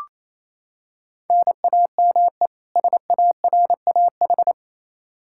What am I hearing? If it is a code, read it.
Morse code, heard as a keyed tone, about 27 wpm, 700 Hz. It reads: NAME SARAH
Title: K1USN SST QSO Practice 28wpm